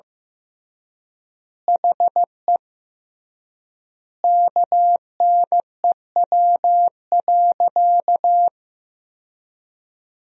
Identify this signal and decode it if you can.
Morse code, heard as a keyed tone, about 15 wpm, 700 Hz. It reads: HE KNEW.
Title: 3x Word-Spacing - Sentences from Top 400 Words - 15wpm